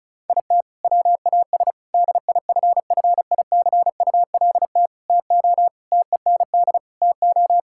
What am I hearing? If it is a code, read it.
Morse code, heard as a keyed tone, about 34 wpm, 700 Hz. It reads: IT WAS DIFFICULT TO TEND TO
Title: Sentences from Top 1000 Words - Review Part 5 - 35wpm